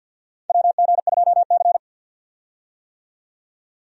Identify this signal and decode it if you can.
Morse code, heard as a keyed tone, about 49 wpm, 700 Hz. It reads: WG2C
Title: USA Call Signs - 4 Characters - 50wpm